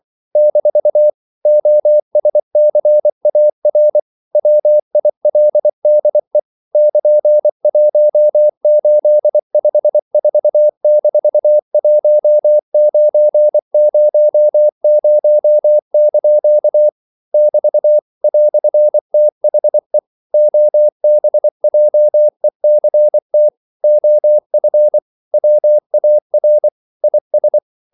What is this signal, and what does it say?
Morse code, heard as a keyed tone, about 24 wpm, 600 Hz. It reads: - OSCAR WILDE <KN>1854-1900) <BT> "THE OBJECT OF WAR IS
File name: CW / Practice / EnglishQuotations0324